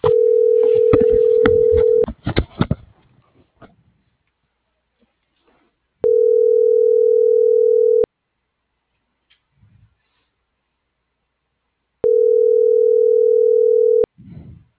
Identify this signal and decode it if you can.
Ringback tone (North America), 440 + 480 Hz.